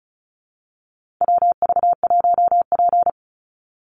Morse code, heard as a keyed tone, about 34 wpm, 700 Hz. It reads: WV1P
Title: USA Call Signs 35wpm